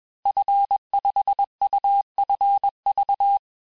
Morse code, heard as a keyed tone, about 21 wpm, 800 Hz. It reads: F5UFV
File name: teste-FIN-de-QSO-vitesse-20mpm